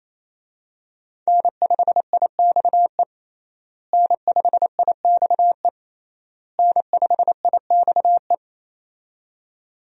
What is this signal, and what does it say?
Morse code, heard as a keyed tone, about 28 wpm, 700 Hz. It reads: N5IXE N5IXE N5IXE
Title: USA Call Signs - 5 Characters - Repeated 3x - 28wpm